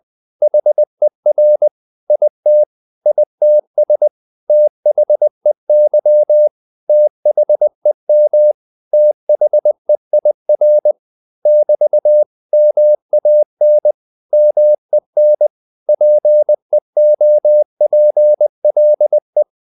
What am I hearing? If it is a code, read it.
Morse code, heard as a keyed tone, about 20 wpm, 600 Hz. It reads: HER IT ITS THEY THEM THEIR <BT> MAN MEN PEOPLE
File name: CW / Practice / 500MostCommonEnglishWords20